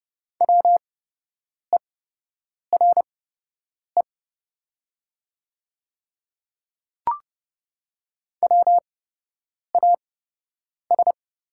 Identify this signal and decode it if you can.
Morse code, heard as a keyed tone, about 30 wpm, spaced as for 8 wpm, 700 Hz. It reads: WERE WAS
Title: 8x Character Spacing - Top 300 Words - 30wpm